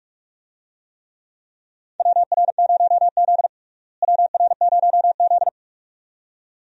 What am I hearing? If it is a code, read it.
Morse code, heard as a keyed tone, about 42 wpm, 700 Hz. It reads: WR0Z WR0Z
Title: USA Call Signs - 4 Characters - Repeated 2x - 45wpm